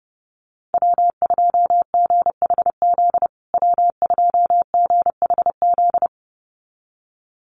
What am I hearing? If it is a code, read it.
Morse code, heard as a keyed tone, about 30 wpm, 700 Hz. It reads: W2GHZ W2GHZ
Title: USA Call Signs Repeated 2x 30wpm